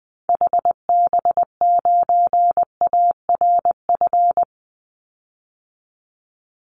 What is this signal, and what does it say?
Morse code, heard as a keyed tone, about 20 wpm, 700 Hz. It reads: HB9ARF
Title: Call Signs Repeated 3x - 20wpm